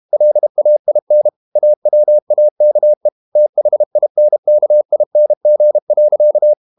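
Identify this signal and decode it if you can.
Morse code, heard as a keyed tone, about 32 wpm, 600 Hz. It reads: LAIN AWAKE THINKING.
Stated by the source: Morse code practice at 32 wpm from CW med Gnister, reading Ken Follett's novel Never in English.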